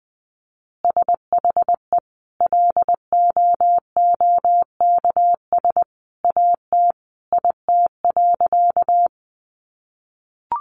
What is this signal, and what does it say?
Morse code, heard as a keyed tone, about 20 wpm, 700 Hz. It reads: SHE LOOKS AT IT.
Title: Additional Easy Phrases 20wpm